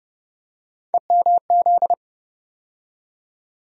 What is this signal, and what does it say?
Morse code, heard as a keyed tone, about 30 wpm, 700 Hz. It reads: EMZ
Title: USA Call Sign Postfixes - 3 Letters 30wpm